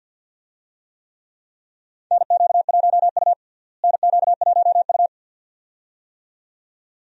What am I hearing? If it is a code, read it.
Morse code, heard as a keyed tone, about 52 wpm, 700 Hz. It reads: NQ1U NQ1U
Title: USA Call Signs - 4 Characters - Repeated 2x - 50wpm